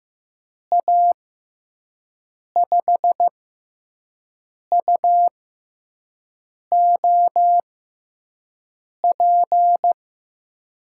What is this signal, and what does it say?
Morse code, heard as a keyed tone, about 15 wpm, spaced as for 5 wpm, 700 Hz. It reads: A5UOP
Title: USA Call Signs - 6 Characters - 6x Character Spacing - 15wpm